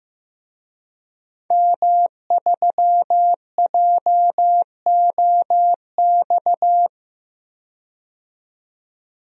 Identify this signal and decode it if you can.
Morse code, heard as a keyed tone, about 15 wpm, 700 Hz. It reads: M3JOX
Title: DXCC-100 Call Signs - 5 Characters - 15wpm